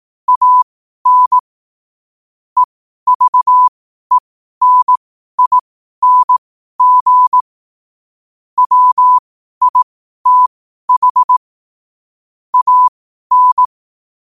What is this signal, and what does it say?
Morse code, heard as a keyed tone, about 18 wpm, spaced as for 12 wpm, 1000 Hz. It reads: AN EVENING WITH AN
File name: qotdfiles / 12wpm